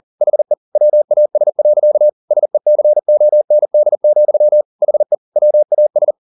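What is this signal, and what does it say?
Morse code, heard as a keyed tone, about 40 wpm, 600 Hz. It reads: HE WAS. SECOND, HE WAS
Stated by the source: Morse code at 40 wpm, practice text, CW med Gnister.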